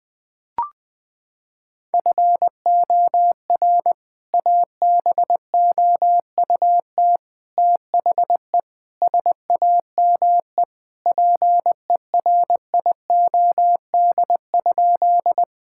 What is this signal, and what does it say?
Morse code, heard as a keyed tone, about 20 wpm, 700 Hz. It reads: FOR ABOUT THE SAME PERIOD?
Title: Sentences from Top 1000 Words - Review Part 5 - 20wpm